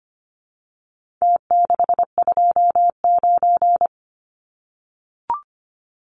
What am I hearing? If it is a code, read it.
Morse code, heard as a keyed tone, about 25 wpm, 700 Hz. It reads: T629
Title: Numbers - Contest Style - 4-Digit 25wpm